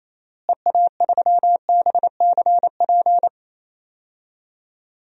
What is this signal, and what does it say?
Morse code, heard as a keyed tone, about 28 wpm, 700 Hz. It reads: EA3BCP
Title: DXCC-100 Call Signs - 6 Characters - Repeated 3x - 28wpm